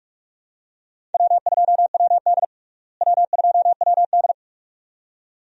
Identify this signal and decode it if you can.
Morse code, heard as a keyed tone, about 46 wpm, 700 Hz. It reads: W2WD W2WD
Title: USA Call Signs - 4 Characters - Repeated 2x - 45wpm